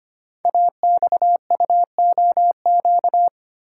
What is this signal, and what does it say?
Morse code, heard as a keyed tone, about 25 wpm, 700 Hz. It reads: AXUOQ
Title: Code Groups - 5 Letters - 25wpm